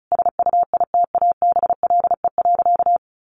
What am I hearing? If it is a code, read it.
Morse code, heard as a keyed tone, about 34 wpm, 700 Hz. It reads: SUITABLE.
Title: Sentences from Top 1400 Words 35wpm